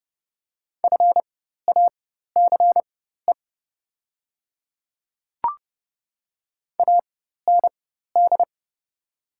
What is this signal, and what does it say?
Morse code, heard as a keyed tone, about 30 wpm, spaced as for 14 wpm, 700 Hz. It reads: FACE AND
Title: 4x Character Spacing - Top 300 Words - 30wpm